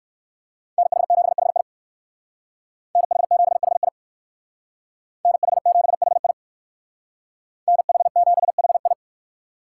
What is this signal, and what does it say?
Morse code, heard as a keyed tone, about 62 wpm, 700 Hz. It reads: NH7HI NH7HI NH7HI NH7HI